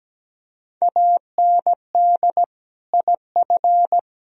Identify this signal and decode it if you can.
Morse code, heard as a keyed tone, about 19 wpm, 700 Hz. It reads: AND IF
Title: Speed-Racing - Sets of 3 Words 17wpm